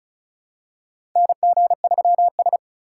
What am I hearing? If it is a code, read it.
Morse code, heard as a keyed tone, about 36 wpm, 700 Hz. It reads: NG3S
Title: USA Call Signs - 4 Characters - 2x Character Spacing - 35wpm